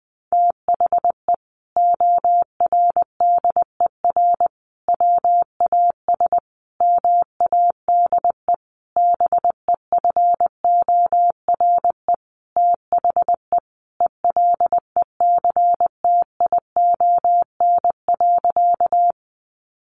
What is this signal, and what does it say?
Morse code, heard as a keyed tone, about 20 wpm, 700 Hz. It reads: THE ORDER WAS MADE BEFORE THE ELECTION.
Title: Sentences from Top 1500 Words 20wpm